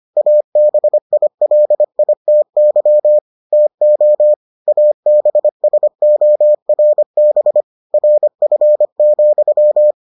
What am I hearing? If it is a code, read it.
Morse code, heard as a keyed tone, about 25 wpm, 600 Hz. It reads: ABILITY TO ABSORB RF,